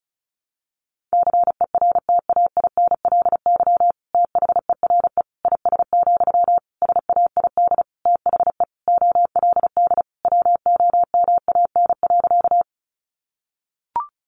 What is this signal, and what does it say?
Morse code, heard as a keyed tone, about 34 wpm, 700 Hz. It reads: CERTAINLY THERE IS, SAID THE OLD WOMAN.
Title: Sentences from Top 1100 Words 35wpm